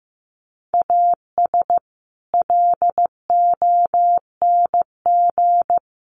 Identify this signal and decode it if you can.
Morse code, heard as a keyed tone, about 15 wpm, 700 Hz. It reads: AS LONG
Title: Sets of 4 Words 15wpm